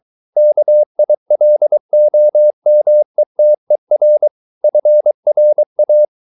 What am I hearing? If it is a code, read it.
Morse code, heard as a keyed tone, about 23 wpm, 600 Hz. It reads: KILOMETER FRA
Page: Ferie 23 wpm